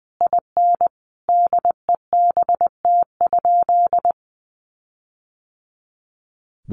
Morse code, heard as a keyed tone, about 20 wpm, 700 Hz. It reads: IN DEBT?
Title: Sentences from Top 1300 Words 20wpm